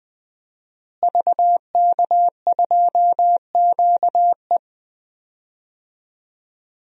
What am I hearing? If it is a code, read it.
Morse code, heard as a keyed tone, about 20 wpm, 700 Hz. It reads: VK2QE
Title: DXCC-100 Call Signs - 5 Characters - Repeated 3x - 20wpm